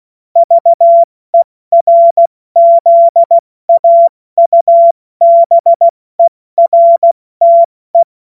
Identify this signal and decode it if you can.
Morse code, heard as a keyed tone, about 16 wpm, 680 Hz. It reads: VERZAUBERTE
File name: Morsekurs / Witze / Witz 05. April 2022 70 F80